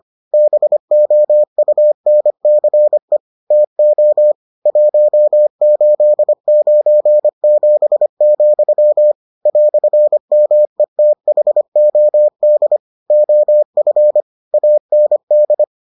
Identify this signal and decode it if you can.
Morse code, heard as a keyed tone, about 25 wpm, 600 Hz. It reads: BOUNCE TO 1897, "METHOD OF AND